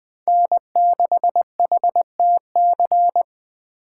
Morse code, heard as a keyed tone, about 21 wpm, 700 Hz. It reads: N6HTC
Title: Speed Racing - USA 1x3 Call Signs 17wpm